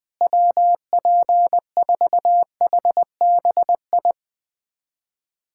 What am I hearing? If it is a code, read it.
Morse code, heard as a keyed tone, about 20 wpm, 700 Hz. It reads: WP4HBI
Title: USA Callsign DE Callsign 20wpm